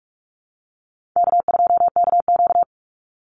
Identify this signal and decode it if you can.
Morse code, heard as a keyed tone, about 44 wpm, 700 Hz. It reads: K2KQ
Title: Call Signs Repeated 3x - 45wpm